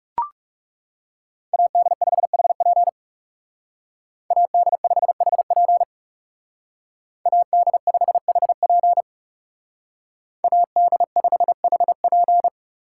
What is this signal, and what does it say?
Morse code, heard as a keyed tone, about 37 wpm, 700 Hz. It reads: AD5HP AD5HP AD5HP AD5HP
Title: Speed Racing - USA 2x2 Call Signs 30wpm